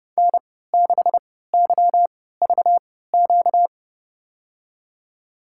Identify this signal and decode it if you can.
Morse code, heard as a keyed tone, about 29 wpm, spaced as for 17 wpm, 700 Hz. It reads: N6YVQ
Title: USA Call Signs - 5 Characters - 3x Character Spacing - 30wpm